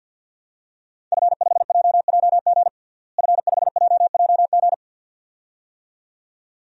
Morse code, heard as a keyed tone, about 47 wpm, 700 Hz. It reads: F5JJG F5JJG
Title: DXCC-100 Call Signs - 5 Characters - Repeated 2x - 50wpm